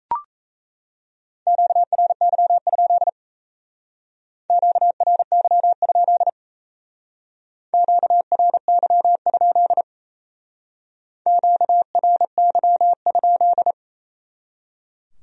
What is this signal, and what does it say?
Morse code, heard as a keyed tone, about 35 wpm, 700 Hz. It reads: QRY? QRY? QRY? QRY?